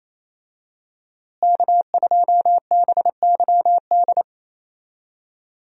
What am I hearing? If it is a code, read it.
Morse code, heard as a keyed tone, about 28 wpm, 700 Hz. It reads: K2BYD
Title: USA Call Signs - 5 Characters - Repeated 2x - 28wpm